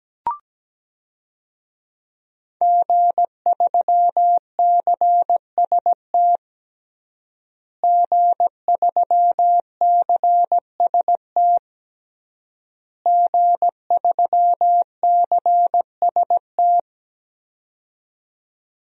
Morse code, heard as a keyed tone, about 17 wpm, 700 Hz. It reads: G3CST G3CST G3CST
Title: DXCC-100 Call Signs - 5 Characters - Repeated 3x - 17wpm